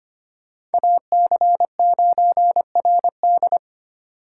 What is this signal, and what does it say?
Morse code, heard as a keyed tone, about 32 wpm, 700 Hz. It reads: AC9RD